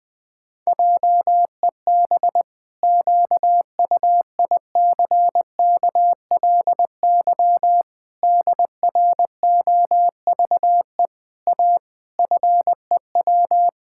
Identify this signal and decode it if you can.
Morse code, heard as a keyed tone, about 20 wpm, 700 Hz. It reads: JEB QUICKLY DROVE A FEW
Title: Pangram Challenge v2 - 20wpm